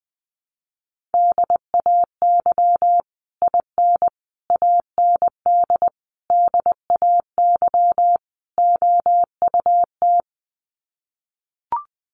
Morse code, heard as a keyed tone, about 19 wpm, 700 Hz. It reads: DAY IN AND DAY OUT
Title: Sets of 5 Words Encore2 20wpm